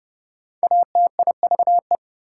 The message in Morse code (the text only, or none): ATIVE